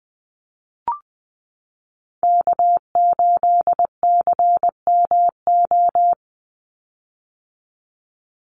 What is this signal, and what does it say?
Morse code, heard as a keyed tone, about 20 wpm, 700 Hz. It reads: K8CMO
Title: Call Signs - 20wpm